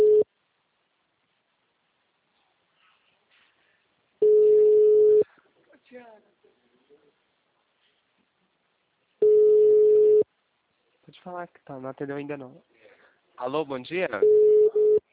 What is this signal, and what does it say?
Ringback tone (Europe, 425 Hz), with speech.